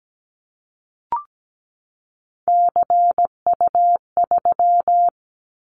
Morse code, heard as a keyed tone, about 17 wpm, 700 Hz. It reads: CU3
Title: Callsign Prefixes for Top 100 DXCC - 17wpm